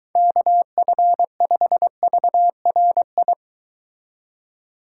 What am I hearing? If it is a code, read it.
Morse code, heard as a keyed tone, about 23 wpm, 700 Hz. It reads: KF5VRI